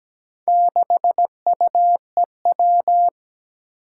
Morse code, heard as a keyed tone, about 17 wpm, 700 Hz. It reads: 6UEW
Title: Four Challenging Letter-Number Combinations 17wpm